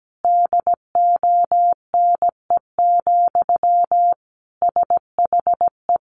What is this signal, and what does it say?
Morse code, heard as a keyed tone, about 17 wpm, 700 Hz. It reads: DONE, SHE
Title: Sentences from Top 1200 Words 17wpm